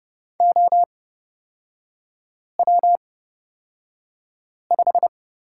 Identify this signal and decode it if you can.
Morse code, heard as a keyed tone, about 31 wpm, spaced as for 5 wpm, 690 Hz. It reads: OW5
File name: Lesson 37e - Letter x - Three Characters 30wpm